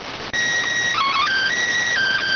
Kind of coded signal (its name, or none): SSTV (slow-scan television)